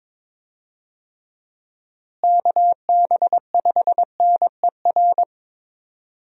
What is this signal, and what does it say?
Morse code, heard as a keyed tone, about 22 wpm, 700 Hz. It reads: KB5NER
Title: USA Call Signs - 6 Characters - Repeated 2x - 22wpm